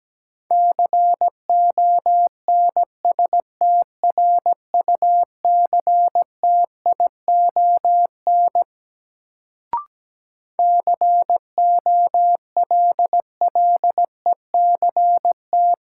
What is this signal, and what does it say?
Morse code, heard as a keyed tone, about 17 wpm, 700 Hz. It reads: CONSTRUCTION COLLECT
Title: Sentences from Top 1600 Words - New Word Drill - 17wpm